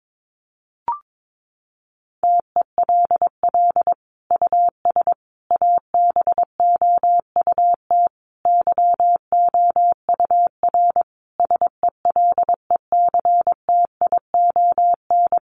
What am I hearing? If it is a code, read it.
Morse code, heard as a keyed tone, about 22 wpm, 700 Hz. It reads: TELL US ABOUT YOUR SELECTION
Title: Sentences from Top 1400 Words 22wpm